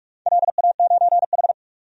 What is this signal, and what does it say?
Morse code, heard as a keyed tone, about 41 wpm, 700 Hz. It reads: LA9H
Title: DXCC-100 Call Signs - 4 Characters - 45wpm